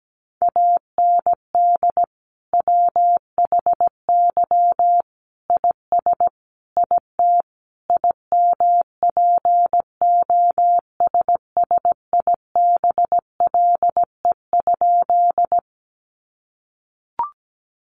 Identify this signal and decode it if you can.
Morse code, heard as a keyed tone, about 17 wpm, 700 Hz. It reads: AND WHY IS IT IMPOSSIBLE?
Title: Sentences from Top 1100 Words 17wpm